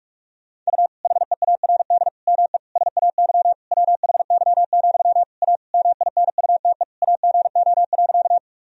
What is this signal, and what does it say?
Morse code, heard as a keyed tone, about 47 wpm, 700 Hz. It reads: U HEARD ME SAY WHY, A MINUTE AGO.
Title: Sentences from Top 500 Words and Abbreviations 45wpm